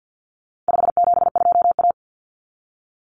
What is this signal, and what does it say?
Morse code, heard as a keyed tone, about 48 wpm, 700 Hz. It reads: 572A
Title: Common Cut Numbers - 4-Digit 50wpm